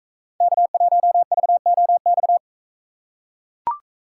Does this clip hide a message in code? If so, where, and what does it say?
Yes: Morse code from 0.4 to 2.4 s, heard as a keyed tone, about 44 wpm, 700 Hz. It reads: K1VOX